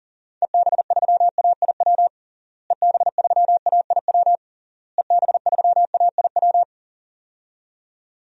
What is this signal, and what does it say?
Morse code, heard as a keyed tone, about 40 wpm, 700 Hz. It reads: EB3AIW EB3AIW EB3AIW
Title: DXCC-100 Call Signs - 6 Characters - Repeated 3x - 40wpm